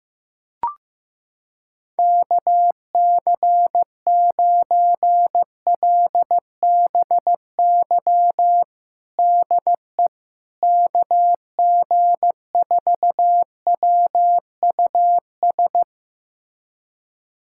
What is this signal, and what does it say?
Morse code, heard as a keyed tone, about 16 wpm, 700 Hz. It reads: KC9LBY DE KG4WUS
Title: USA Callsign DE Callsign - 6 Characters - 15wpm